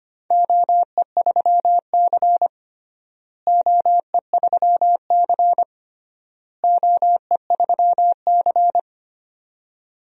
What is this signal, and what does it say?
Morse code, heard as a keyed tone, about 25 wpm, 700 Hz. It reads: OE3C OE3C OE3C
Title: DXCC-100 Call Signs - 4 Characters - Repeated 3x - 25wpm